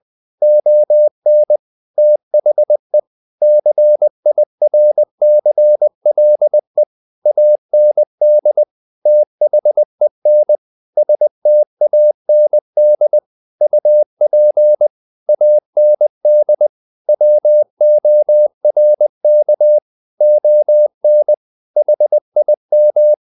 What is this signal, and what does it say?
Morse code, heard as a keyed tone, about 20 wpm, 600 Hz. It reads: ON THE CIRCLE AND THEN STAND UP AND WORK ON HIM
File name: the old man and the sea part 2 20wpm